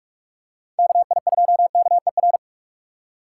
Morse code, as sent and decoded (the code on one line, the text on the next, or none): -.- .. ..--- -.- . .-.
KI2KER